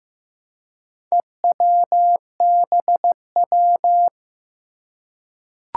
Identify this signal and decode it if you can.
Morse code, heard as a keyed tone, about 15 wpm, 700 Hz. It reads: EWBW
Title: Four Challenging Letter-Number Combinations 15wpm